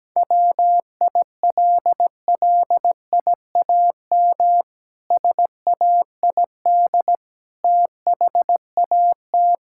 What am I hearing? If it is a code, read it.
Morse code, heard as a keyed tone, about 17 wpm, 700 Hz. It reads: WILLIAM SAID THAT
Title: Pangram Challenge v2 - 17wpm